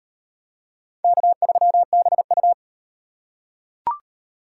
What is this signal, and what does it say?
Morse code, heard as a keyed tone, about 38 wpm, 700 Hz. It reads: K3BU